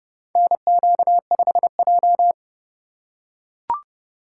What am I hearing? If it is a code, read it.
Morse code, heard as a keyed tone, about 30 wpm, 700 Hz. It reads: NQ5J
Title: USA Call Signs - 2x Character Spacing - 30wpm